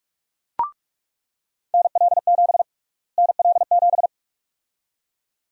Morse code, heard as a keyed tone, about 42 wpm, 700 Hz. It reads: NL7 NL7
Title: USA Call Sign Prefixes - Repeated 2x - 45wpm